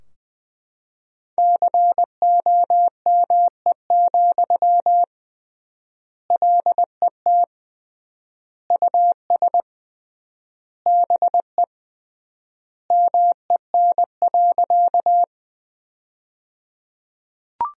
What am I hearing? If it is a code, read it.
Morse code, heard as a keyed tone, about 20 wpm, 700 Hz. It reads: COME, LET US BE MEN.
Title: 3x Word-Spacing - Sentences from Top 300 Words - 20wpm